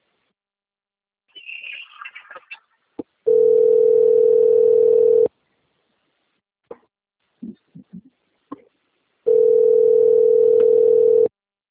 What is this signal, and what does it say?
Ringback tone (North America), 440 + 480 Hz.